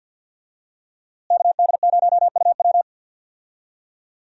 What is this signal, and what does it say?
Morse code, heard as a keyed tone, about 48 wpm, 700 Hz. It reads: KD0UW
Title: USA Call Signs - 5 Characters - Repeated 2x - 50wpm